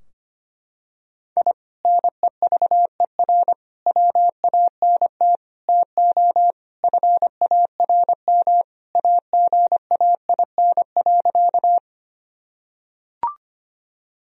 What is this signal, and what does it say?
Morse code, heard as a keyed tone, about 25 wpm, 700 Hz. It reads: I NEVER WANT TO FARM AGAIN.
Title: Sentences from Top 1000 Words - Review Part 2 - 25wpm